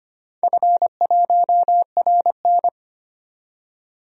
Morse code, heard as a keyed tone, about 25 wpm, 700 Hz. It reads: F1RN
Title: DXCC-100 Call Signs - 4 Characters - 25wpm